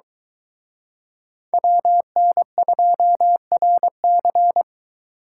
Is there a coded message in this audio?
Yes: Morse code from 1.5 to 4.6 s, heard as a keyed tone, about 22 wpm, 700 Hz. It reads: WN2RC